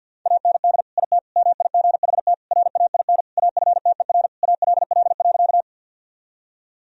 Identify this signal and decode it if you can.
Morse code, heard as a keyed tone, about 49 wpm, 700 Hz. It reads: AND IT MIGHT RAIN AFTER ALL.
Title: Sentences from Top 1000 Words - Review Part 1 - 50wpm